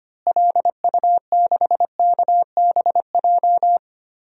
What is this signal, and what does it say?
Morse code, heard as a keyed tone, about 25 wpm, 690 Hz. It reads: LU6KBJ
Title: DXCC-100 Call Signs - 6 Characters - Repeated 3x - 25wpm